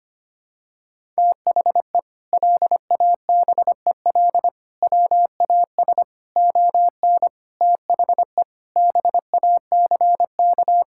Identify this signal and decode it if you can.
Morse code, heard as a keyed tone, about 25 wpm, 700 Hz. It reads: THE LABEL WAS ON THE BACK
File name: practice / 25wpm / Sentences with 4-Word Phrases - 25wpm